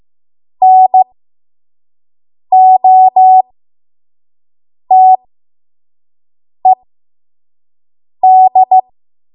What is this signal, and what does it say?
Morse code, heard as a keyed tone, about 15 wpm, spaced as for 5 wpm, 750 Hz. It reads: NOTED
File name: Morse / 05WPM